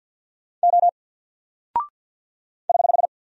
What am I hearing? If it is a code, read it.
Morse code, heard as a keyed tone, about 48 wpm, 700 Hz. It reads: O <HH>
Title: Single Character (Letter:Number:Punctuation:Prosign) 50wpm